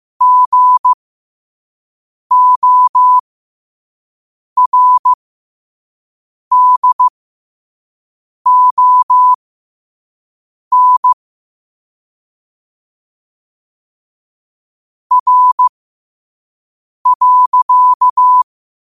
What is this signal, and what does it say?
Morse code, heard as a keyed tone, about 15 wpm, spaced as for 5 wpm, 1000 Hz. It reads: GORDON R.
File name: qotdfiles / 5wpm